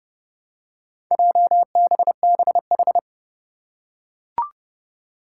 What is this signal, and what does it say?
Morse code, heard as a keyed tone, about 30 wpm, 700 Hz. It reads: JBBH
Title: Four Challenging Letter-Number Combinations 30wpm